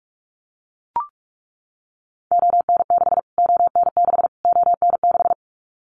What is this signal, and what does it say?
Morse code, heard as a keyed tone, about 44 wpm, 700 Hz. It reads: ON6 ON6 ON6
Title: Callsign Prefixes for Top 25 DXCC - Repeated 3x - 45wpm